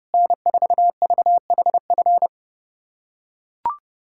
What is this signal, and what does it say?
Morse code, heard as a keyed tone, about 31 wpm, 700 Hz. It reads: N4VHF
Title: USA Callsign DE Callsign - 5 Characters - 30wpm